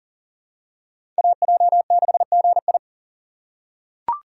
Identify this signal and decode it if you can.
Morse code, heard as a keyed tone, about 40 wpm, 700 Hz. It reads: AJ6GI
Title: USA Call Signs - 5 Characters - 2x Character Spacing - 40wpm